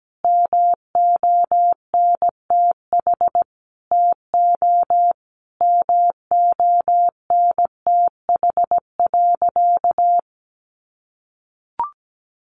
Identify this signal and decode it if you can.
Morse code, heard as a keyed tone, about 17 wpm, 700 Hz. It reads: MONTH TO MONTH.
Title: Sentences from Top 1200 Words 17wpm